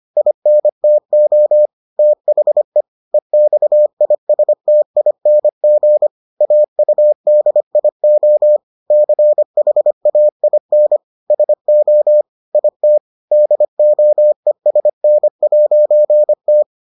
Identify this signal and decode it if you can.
Morse code, heard as a keyed tone, about 25 wpm, 600 Hz. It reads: INTO THE EXISTING AUDIO CHAIN SO IT DOESN'T